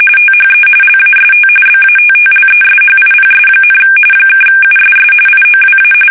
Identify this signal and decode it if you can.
RTTY, about 45.45 baud. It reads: THE LAST BATTLESHIPS, (USS IOWA
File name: fsk850 / battleships